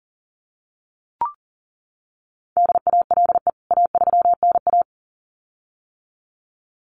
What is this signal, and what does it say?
Morse code, heard as a keyed tone, about 40 wpm, 700 Hz. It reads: DALE A3NA
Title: CWT Exchange 40wpm